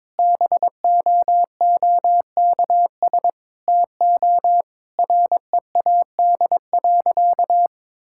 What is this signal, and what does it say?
Morse code, heard as a keyed tone, about 22 wpm, 700 Hz. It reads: BOOKS TO READ.
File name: practice / 22wpm / Sentences with 4-Word Phrases - 22wpm